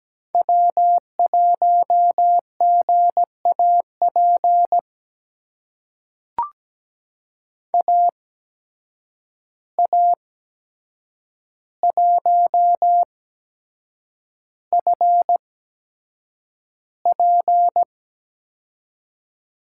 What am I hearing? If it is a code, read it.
Morse code, heard as a keyed tone, about 17 wpm, spaced as for 7 wpm, 700 Hz. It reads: W1GAP AA1FP
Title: USA Call Signs - 5 Characters - 8x Character Spacing - 17wpm